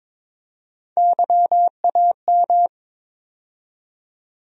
Morse code, heard as a keyed tone, about 22 wpm, 700 Hz. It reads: YAM
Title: USA Call Sign Postfixes - 3 Letters 22wpm